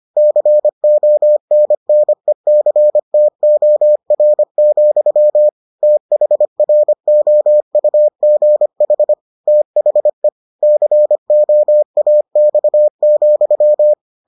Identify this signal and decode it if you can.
Morse code, heard as a keyed tone, about 25 wpm, 600 Hz. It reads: CONNECTOR, THROUGH THE COAX,